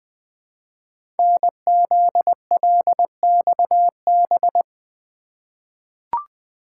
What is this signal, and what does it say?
Morse code, heard as a keyed tone, about 20 wpm, 700 Hz. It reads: NZLXB
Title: Code Groups - 5 Letters - 20wpm